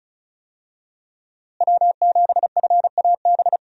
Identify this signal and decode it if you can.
Morse code, heard as a keyed tone, about 36 wpm, 700 Hz. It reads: W7FAB
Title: USA Call Signs - 5 Characters - Repeated 3x - 35wpm